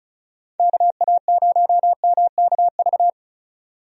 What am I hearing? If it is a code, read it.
Morse code, heard as a keyed tone, about 36 wpm, 700 Hz. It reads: KA0MKV
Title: USA Call Signs - 6 Characters - 35wpm